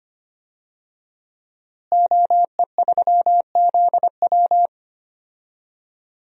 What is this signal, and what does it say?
Morse code, heard as a keyed tone, about 25 wpm, 700 Hz. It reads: OE3ZW
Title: DXCC-100 Call Signs - 5 Characters - Repeated 3x - 25wpm